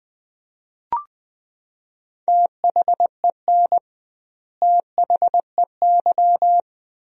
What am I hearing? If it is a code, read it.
Morse code, heard as a keyed tone, about 20 wpm, 700 Hz. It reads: THEN THEY
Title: 2x Word-Spacing - Sentences from Top 200 Words - 20wpm